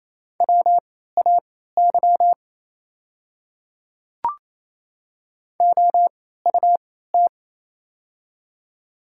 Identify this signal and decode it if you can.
Morse code, heard as a keyed tone, about 27 wpm, spaced as for 16 wpm, 690 Hz. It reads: WAY OUT
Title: 3x Character Spacing - Top 200 Words - 28wpm